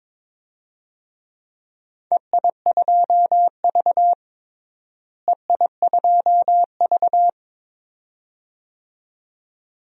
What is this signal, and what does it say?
Morse code, heard as a keyed tone, about 22 wpm, 700 Hz. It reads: EI2V EI2V
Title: DXCC-100 Call Signs - 4 Characters - Repeated 2x - 22wpm